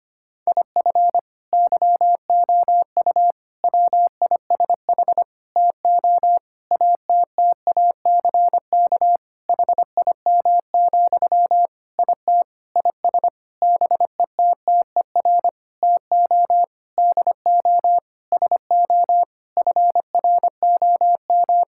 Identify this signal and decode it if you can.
Morse code, heard as a keyed tone, about 25 wpm, 700 Hz. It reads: IF YOU WISH TO ATTACK HIM, IT IS BETTER TO DO SO FROM
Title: Sentences from Top 1600 Words 25wpm